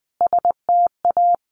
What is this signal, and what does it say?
Morse code, heard as a keyed tone, about 20 wpm, 700 Hz. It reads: STA
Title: Sets of 4 Words 20wpm